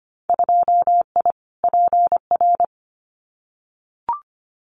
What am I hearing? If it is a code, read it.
Morse code, heard as a keyed tone, about 25 wpm, 700 Hz. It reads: 2I PR
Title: Winter Field Day Exchanges Encore 25wpm